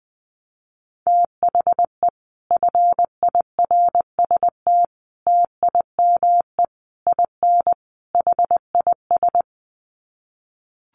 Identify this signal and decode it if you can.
Morse code, heard as a keyed tone, about 19 wpm, 700 Hz. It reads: THE FIRST TIME IN HIS
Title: Sets of 5 Words Encore2 20wpm